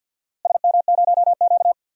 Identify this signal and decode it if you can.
Morse code, heard as a keyed tone, about 51 wpm, 700 Hz. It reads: SM0Q
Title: DXCC-100 Call Signs - 4 Characters - Repeated 2x - 50wpm